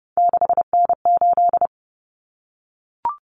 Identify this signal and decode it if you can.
Morse code, heard as a keyed tone, about 30 wpm, 700 Hz. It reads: 6N8
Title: Common Cut Numbers - 3-Digit 30wpm